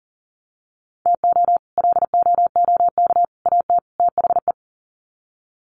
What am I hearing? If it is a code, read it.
Morse code, heard as a keyed tone, about 40 wpm, 700 Hz. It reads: TO LOOK AT THE
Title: Sets of 4 Words 40wpm